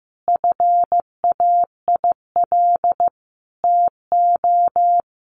Morse code, heard as a keyed tone, about 15 wpm, 700 Hz. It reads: FAIL TO
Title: Sentences from Top 1100 Words 15wpm